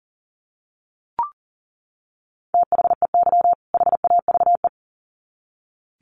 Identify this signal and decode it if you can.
Morse code, heard as a keyed tone, about 40 wpm, 700 Hz. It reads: THEY HAVE
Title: Sets of 2 Words 40wpm